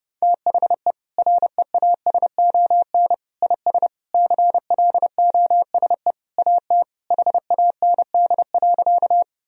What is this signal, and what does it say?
Morse code, heard as a keyed tone, about 30 wpm, 700 Hz. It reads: THE REASON IS CLOSE AT HAND.
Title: Sentences from Top 1000 Words - Review Part 1 - 30wpm